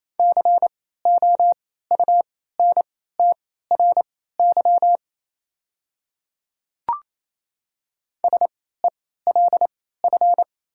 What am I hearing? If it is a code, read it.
Morse code, heard as a keyed tone, about 28 wpm, spaced as for 16 wpm, 700 Hz. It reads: COUNTRY SELF
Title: 3x Character Spacing - Top 300 Words - 28wpm